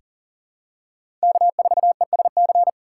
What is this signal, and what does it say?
Morse code, heard as a keyed tone, about 40 wpm, 700 Hz. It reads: K4ESC
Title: USA Call Signs - 5 Characters - 6x Character Spacing - 40wpm